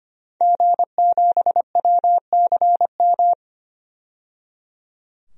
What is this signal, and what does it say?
Morse code, heard as a keyed tone, about 25 wpm, 700 Hz. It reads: G7WCM
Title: DXCC-100 Call Signs - 5 Characters - 25wpm